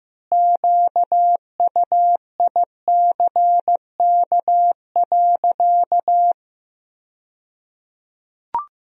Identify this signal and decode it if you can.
Morse code, heard as a keyed tone, about 15 wpm, 700 Hz. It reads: QUICK.
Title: 2x Word-Spacing - Sentences from Top 500 Words - 15wpm